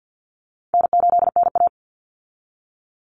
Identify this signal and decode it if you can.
Morse code, heard as a keyed tone, about 50 wpm, 700 Hz. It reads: N8NA
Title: Easy Call Signs Repeated 3x - 50wpm